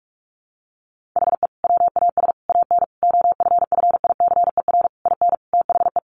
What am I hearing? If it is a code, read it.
Morse code, heard as a keyed tone, about 44 wpm, 700 Hz. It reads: HE WAS AN OFFICER IN THE
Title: Sentences from Top 1200 Words 45wpm